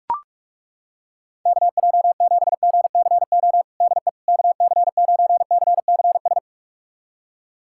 Morse code, heard as a keyed tone, about 41 wpm, 700 Hz. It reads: KJ7GCO DE KC9CCS
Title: USA Callsign DE Callsign 45wpm